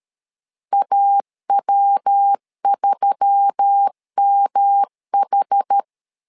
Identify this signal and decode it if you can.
Morse code, heard as a keyed tone, about 13 wpm, 800 Hz. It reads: AW3MH
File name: LEZIONI / 4 Q1S3 12wpm 2023